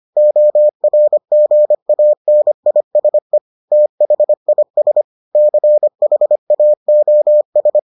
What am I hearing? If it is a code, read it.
Morse code, heard as a keyed tone, about 25 wpm, 600 Hz. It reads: ORGANISE THIS CHAOS